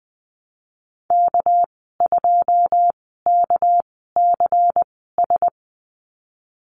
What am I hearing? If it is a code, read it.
Morse code, heard as a keyed tone, about 20 wpm, spaced as for 14 wpm, 700 Hz. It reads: K2KCS
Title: 231 Lesson 37g - Letter x - Call Signs 20wpm